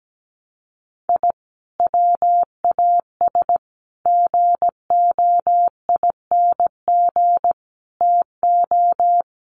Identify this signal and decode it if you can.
Morse code, heard as a keyed tone, about 17 wpm, 700 Hz. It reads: I WAS GOING TO
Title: Sets of 5 Words 17wpm